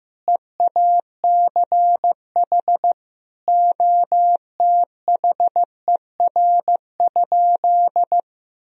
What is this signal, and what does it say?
Morse code, heard as a keyed tone, about 16 wpm, 700 Hz. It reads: EACH OTHER?
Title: Sentences from Top 200 Words and Abbreviations 15wpm